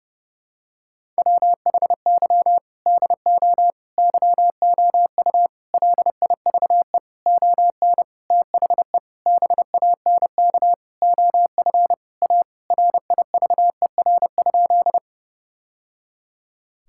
Morse code, heard as a keyed tone, about 30 wpm, 700 Hz. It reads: WHY DO YOU LIVE ON THE BANK OF A RIVER?
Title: Sentences from Top 1000 Words - Review Part 6 - 30wpm